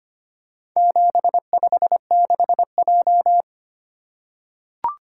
Morse code, heard as a keyed tone, about 25 wpm, 700 Hz. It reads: 756J
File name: practice / 25wpm / Four Letter-Number Combinations - 25wpm